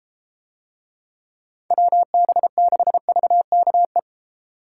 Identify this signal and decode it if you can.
Morse code, heard as a keyed tone, about 32 wpm, 700 Hz. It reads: WB6VKE